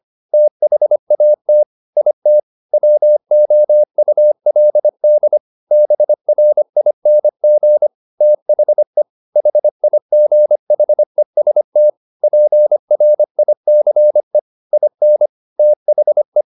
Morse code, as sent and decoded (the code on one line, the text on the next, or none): - .... .- - / .. - / .-- --- ..- .-.. -.. / -... .-. .. -. --. / - .... / .... .. --. .... ... - / .--. .-. .. -.-. / .. -. / - .... .
THAT IT WOULD BRING THE HIGHEST PRICE IN THE